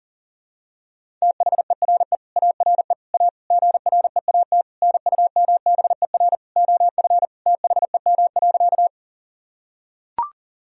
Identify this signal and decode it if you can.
Morse code, heard as a keyed tone, about 37 wpm, 700 Hz. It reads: THERE ARE A GREAT NUMBER OF THEM.
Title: Sentences from Top 1000 Words - Review Part 1 - 40wpm